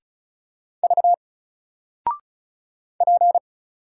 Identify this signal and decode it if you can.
Morse code, heard as a keyed tone, about 36 wpm, 700 Hz. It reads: V P
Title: Single Letter-Number 35wpm